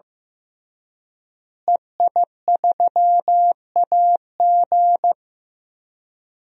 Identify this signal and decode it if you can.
Morse code, heard as a keyed tone, about 15 wpm, 700 Hz. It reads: EI3AG
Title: DXCC-100 Call Signs - 5 Characters - Repeated 3x - 15wpm